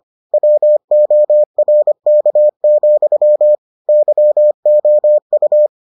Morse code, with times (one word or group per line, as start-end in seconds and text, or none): 0.3-3.5 WORK,
3.9-5.7 YOU